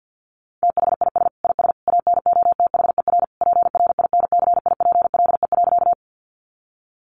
Morse code, heard as a keyed tone, about 50 wpm, 700 Hz. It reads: THIS IS ANOTHER PRINCIPLE.
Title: Sentences from Top 1400 Words 50wpm